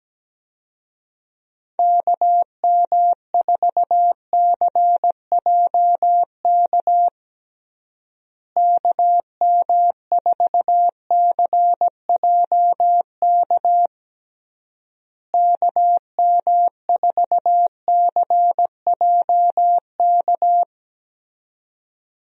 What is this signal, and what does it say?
Morse code, heard as a keyed tone, about 17 wpm, 700 Hz. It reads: KM4CJK KM4CJK KM4CJK
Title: USA Call Signs - 6 Characters - Repeated 3x - 17wpm